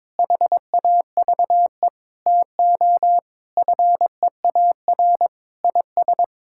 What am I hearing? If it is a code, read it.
Morse code, heard as a keyed tone, about 22 wpm, 700 Hz. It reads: HAVE TO FEAR IS
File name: Sets of 9 Words - 22wpm